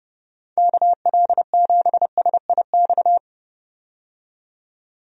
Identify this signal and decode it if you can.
Morse code, heard as a keyed tone, about 30 wpm, 700 Hz. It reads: KL7SIX
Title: DXCC-100 Call Signs - 6 Characters - 30wpm